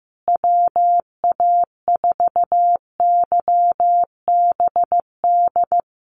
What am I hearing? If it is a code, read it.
Morse code, heard as a keyed tone, about 15 wpm, 700 Hz. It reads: WA4YBD
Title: USA Call Signs Repeated 3x 15wpm